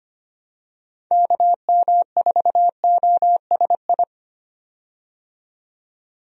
Morse code, heard as a keyed tone, about 25 wpm, 700 Hz. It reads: KM4OSI
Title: USA Call Signs - 6 Characters - 25wpm